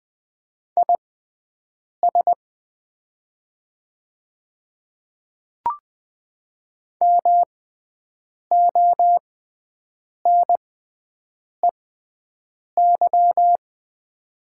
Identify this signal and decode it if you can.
Morse code, heard as a keyed tone, about 20 wpm, spaced as for 7 wpm, 700 Hz. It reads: IS MONEY
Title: 6x Character Spacing - Top 500 Words - 20wpm